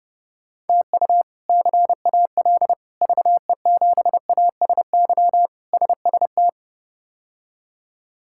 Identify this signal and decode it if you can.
Morse code, heard as a keyed tone, about 32 wpm, 700 Hz. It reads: TU CAL VE7ASY SST
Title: K1USN SST QSO Practice - 30wpm